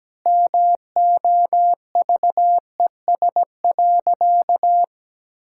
Morse code, heard as a keyed tone, about 17 wpm, 700 Hz. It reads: MOVES.
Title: Pangram Challenge v2 - 17wpm